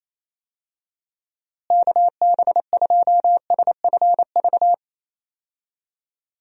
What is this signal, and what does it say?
Morse code, heard as a keyed tone, about 27 wpm, 700 Hz. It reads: KB2SFV
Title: USA Call Signs - 6 Characters - Repeated 2x - 28wpm